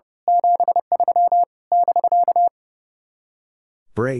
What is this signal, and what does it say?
Morse code, heard as a keyed tone, about 28 wpm, 700 Hz. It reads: 73 <BK>
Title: Speed-Racing Common QSO Phrases - 30wpm